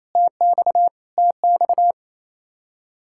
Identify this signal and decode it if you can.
Morse code, heard as a keyed tone, about 28 wpm, 700 Hz. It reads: TX TX
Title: Practice QSO - Exchange 1 - Beginning - 28wpm